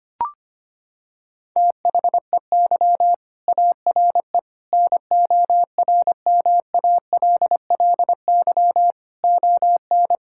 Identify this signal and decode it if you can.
Morse code, heard as a keyed tone, about 25 wpm, 700 Hz. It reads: THEY ARE NORMALLY ON
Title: Sentences from Top 1000 Words - Review Part 6 - 25wpm